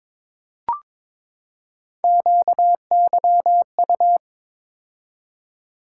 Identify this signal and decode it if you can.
Morse code, heard as a keyed tone, about 22 wpm, 700 Hz. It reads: QYU
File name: practice / 22wpm / Three Letter-Number Combinations - 22wpm